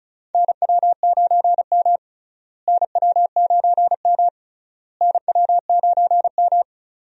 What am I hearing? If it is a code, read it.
Morse code, heard as a keyed tone, about 36 wpm, 700 Hz. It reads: NW9M NW9M NW9M
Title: USA Call Signs - 4 Characters - Repeated 3x - 35wpm